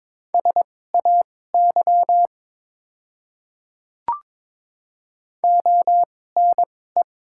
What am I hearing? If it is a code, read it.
Morse code, heard as a keyed tone, about 22 wpm, spaced as for 16 wpm, 700 Hz. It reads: SAY ONE
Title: 2x Character Spacing - Top 300 Words - 22wpm